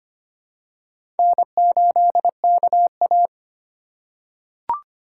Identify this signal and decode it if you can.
Morse code, heard as a keyed tone, about 25 wpm, 700 Hz. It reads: N8KA
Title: USA Call Signs - 4 Characters - 2x Character Spacing - 25wpm